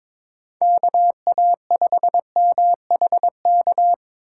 Morse code, heard as a keyed tone, about 22 wpm, 700 Hz. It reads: KA5MHK
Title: USA Call Signs - 6 Characters - 3x Character Spacing - 22wpm